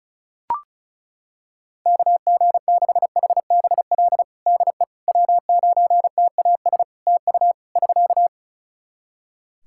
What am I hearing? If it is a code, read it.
Morse code, heard as a keyed tone, about 34 wpm, 700 Hz. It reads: KG6HBL DE W9TAS TU <SK>
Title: Practice QSO - Exchange 4 - Ending - 35wpm